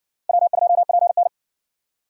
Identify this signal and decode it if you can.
Morse code, heard as a keyed tone, about 60 wpm, 700 Hz. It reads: W2PN